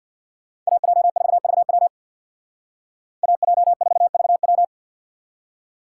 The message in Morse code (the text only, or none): AJ4VW AJ4VW